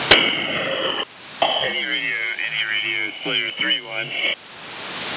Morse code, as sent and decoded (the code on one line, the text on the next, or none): none